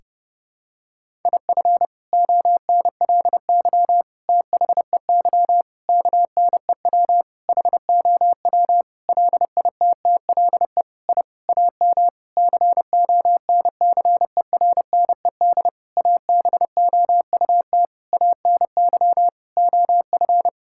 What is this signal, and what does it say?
Morse code, heard as a keyed tone, about 32 wpm, 700 Hz. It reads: IF ONLY THEY KNEW HOW LITTLE I AM CONCERNED ABOUT ANY OF
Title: Sentences from Top 1000 Words - Review Part 6 - 30wpm